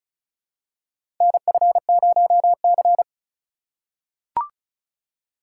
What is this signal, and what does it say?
Morse code, heard as a keyed tone, about 36 wpm, 690 Hz. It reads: NF0C
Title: USA Call Signs - 4 Characters - 2x Character Spacing - 35wpm